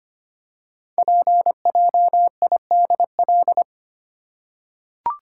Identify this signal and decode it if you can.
Morse code, heard as a keyed tone, about 25 wpm, 700 Hz. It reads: PJIDL
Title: Code Groups - 5 Letters - 25wpm